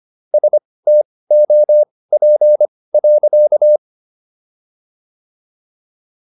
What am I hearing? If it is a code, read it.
Morse code, heard as a keyed tone, about 25 wpm, spaced as for 18 wpm, 600 Hz. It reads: STOP.